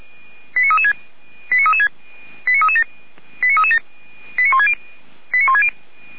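Five-tone selective call (ZVEI), heard as quick runs of five tones, tones of about 70 ms. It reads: 89336 89336 89337 89337 89160 89160
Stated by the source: ZVEI five-tone selective call received on SDR.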